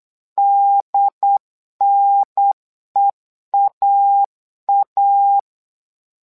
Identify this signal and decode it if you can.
Morse code, heard as a keyed tone, about 8 wpm, 800 Hz. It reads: DNEAA
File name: LEZIONI / 2 UHDN / 2 UHDN 8wpm2232